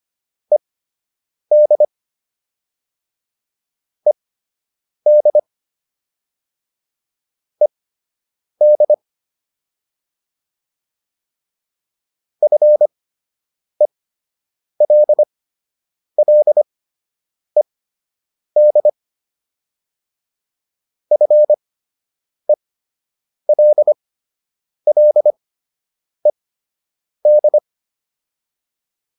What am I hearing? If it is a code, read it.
Morse code, heard as a keyed tone, about 25 wpm, spaced as for 8 wpm, 600 Hz. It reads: ED ED ED FELLED FELLED
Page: fundamental-practice-files